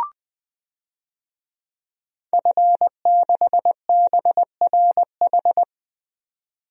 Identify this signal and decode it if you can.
Morse code, heard as a keyed tone, about 20 wpm, 700 Hz. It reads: F6BRH
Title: DXCC-100 Call Signs - 5 Characters - 20wpm